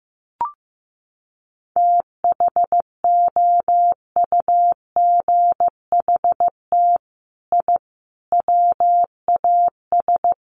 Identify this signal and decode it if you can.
Morse code, heard as a keyed tone, about 15 wpm, 700 Hz. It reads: THOUGHT I WAS
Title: Sets of 5 Words Encore2 15wpm